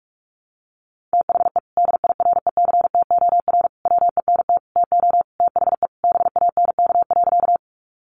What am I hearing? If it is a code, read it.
Morse code, heard as a keyed tone, about 44 wpm, 700 Hz. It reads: THE DIRECTOR WENT TO THE BANK.
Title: Sentences from Top 1300 Words 45wpm